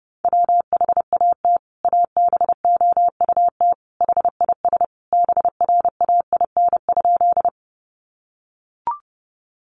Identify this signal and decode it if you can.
Morse code, heard as a keyed tone, about 30 wpm, 700 Hz. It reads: WHAT ABOUT HIS BRAIN?
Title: Sentences from Top 1100 Words 30wpm